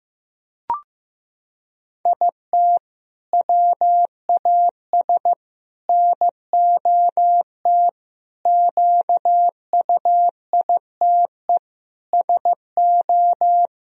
Morse code, heard as a keyed tone, about 15 wpm, 700 Hz. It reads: IT WAS NOT QUITE SO
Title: Sentences from Top 1000 Words - Review Part 2 - 15wpm